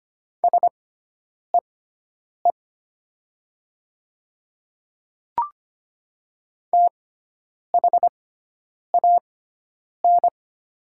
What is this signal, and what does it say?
Morse code, heard as a keyed tone, about 25 wpm, spaced as for 9 wpm, 700 Hz. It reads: SEE THAN
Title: 6x Character Spacing - Top 100 Words - 25wpm